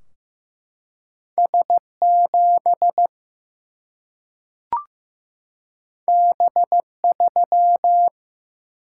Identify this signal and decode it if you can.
Morse code, heard as a keyed tone, about 15 wpm, 700 Hz. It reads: S7 B3
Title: Single-Letter Single-Number 15wpm